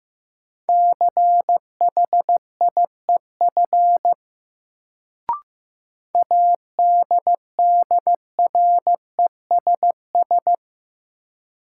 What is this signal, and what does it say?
Morse code, heard as a keyed tone, about 15 wpm, 700 Hz. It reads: CHIEF ADDRESS
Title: Sets of 5 Words Encore - New Word Drill - 15wpm